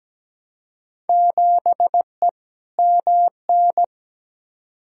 Morse code, heard as a keyed tone, about 17 wpm, 700 Hz. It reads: 7E MN
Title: Field Day Exchanges Encore - 17wpm